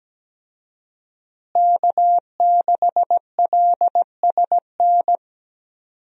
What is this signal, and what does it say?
Morse code, heard as a keyed tone, about 17 wpm, 700 Hz. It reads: K6LSN
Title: USA Call Signs - 5 Characters - Repeated 2x - 17wpm